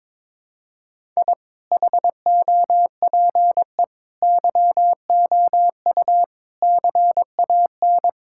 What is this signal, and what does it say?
Morse code, heard as a keyed tone, about 22 wpm, 700 Hz. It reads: I HOPE YOU CAN
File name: practice / 22wpm / Sets of 8 Words - 22wpm